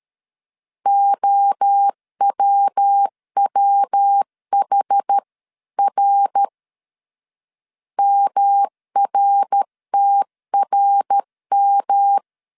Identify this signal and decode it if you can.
Morse code, heard as a keyed tone, about 13 wpm, 800 Hz. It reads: OWWHR MRTRM
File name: CW / DET / 1 3 1500C S A 12wpm 2905